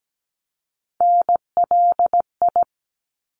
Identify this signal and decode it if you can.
Morse code, heard as a keyed tone, about 17 wpm, 700 Hz. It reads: NLI
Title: ARRL Sections 17wpm